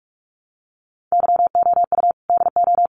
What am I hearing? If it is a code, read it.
Morse code, heard as a keyed tone, about 44 wpm, 700 Hz. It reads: YOU DO
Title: Sets of 5 Words Encore 45wpm